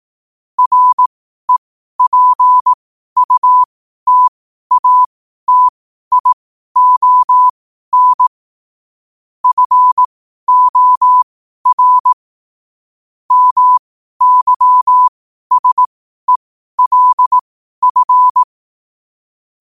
Morse code, heard as a keyed tone, about 18 wpm, spaced as for 12 wpm, 1000 Hz. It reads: REPUTATION FOR MYSELF
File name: qotdfiles / 12wpm